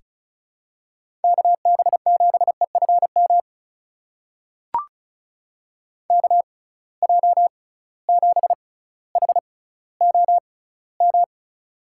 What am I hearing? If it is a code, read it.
Morse code, heard as a keyed tone, about 34 wpm, spaced as for 18 wpm, 700 Hz. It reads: KB7EFM KJ7HOM
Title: USA Call Signs - 6x Character Spacing - 35wpm